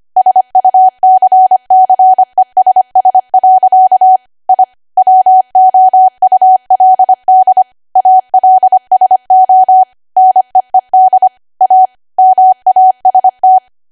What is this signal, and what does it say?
Morse code, heard as a keyed tone, about 25 wpm, 750 Hz. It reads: SUCCESS. I WOULD ALSO NEED A MAST
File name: Archive / 25 WPM / 25WPM